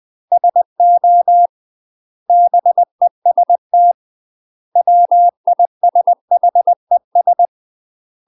Morse code, heard as a keyed tone, about 20 wpm, 700 Hz. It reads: SO BEST WISHES
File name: UK MASTER QSOs TRAINING / 20WPM / UK MASTER QSOs 0065